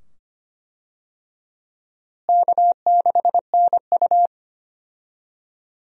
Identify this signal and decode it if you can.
Morse code, heard as a keyed tone, about 25 wpm, 700 Hz. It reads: K6NU
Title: USA Call Signs - 4 Characters - Repeated 2x - 25wpm